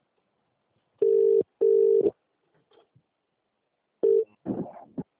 Ringback tone (UK), 400 + 450 Hz.